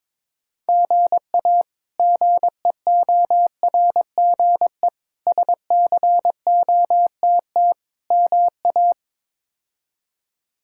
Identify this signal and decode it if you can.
Morse code, heard as a keyed tone, about 22 wpm, 700 Hz. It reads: GA GEORGE SCOTT MA
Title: K1USN SST QSO Practice - 22wpm